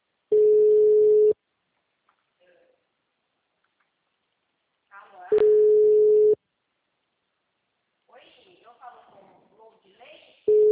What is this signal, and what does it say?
Ringback tone (Europe, 425 Hz), with speech.
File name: out- -2029- - - .37702